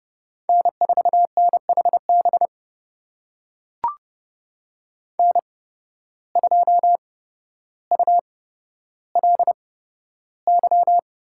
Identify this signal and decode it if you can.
Morse code, heard as a keyed tone, about 29 wpm, spaced as for 13 wpm, 700 Hz. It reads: N4NHB N2ULY
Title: USA Call Signs - 5 Characters - 8x Character Spacing - 30wpm